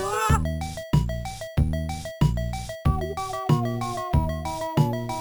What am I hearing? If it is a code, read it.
There is no code.